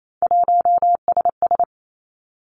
Morse code, heard as a keyed tone, about 28 wpm, 700 Hz. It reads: 1SS
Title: Three Challenging Letter-Number Combinations 28wpm